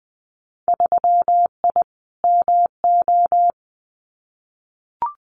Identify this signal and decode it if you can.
Morse code, heard as a keyed tone, about 20 wpm, 700 Hz. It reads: 3I MO
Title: Winter Field Day Exchanges 20wpm